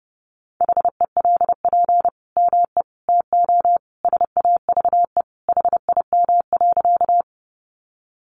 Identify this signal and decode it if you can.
Morse code, heard as a keyed tone, about 30 wpm, 700 Hz. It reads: HELP ME TO SAVE HIM.
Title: Sentences from Top 1100 Words 30wpm